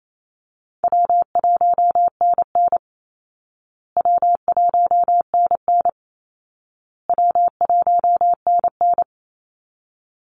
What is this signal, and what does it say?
Morse code, heard as a keyed tone, about 28 wpm, 700 Hz. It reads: W1NN W1NN W1NN
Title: Easy Call Signs Repeated 3x - 28wpm